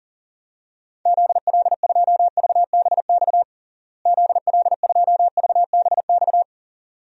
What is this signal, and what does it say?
Morse code, heard as a keyed tone, about 40 wpm, 700 Hz. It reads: ZL2VBX ZL2VBX
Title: DXCC-100 Call Signs - 6 Characters - Repeated 2x - 40wpm